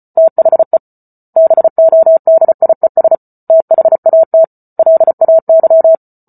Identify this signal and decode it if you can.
Morse code, heard as a keyed tone, about 34 wpm, 650 Hz. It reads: THE BODIES THAT LAY